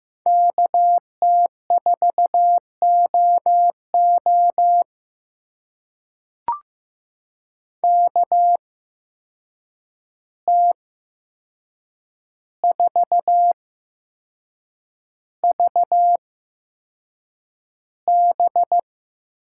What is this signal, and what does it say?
Morse code, heard as a keyed tone, about 15 wpm, spaced as for 6 wpm, 700 Hz. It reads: KT4OO KT4VB
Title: USA Call Signs - 5 Characters - 8x Character Spacing - 15wpm